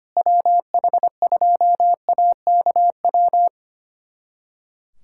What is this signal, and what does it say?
Morse code, heard as a keyed tone, about 25 wpm, 700 Hz. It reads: WH2AKW
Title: USA Callsign DE Callsign - 6 Characters - 25wpm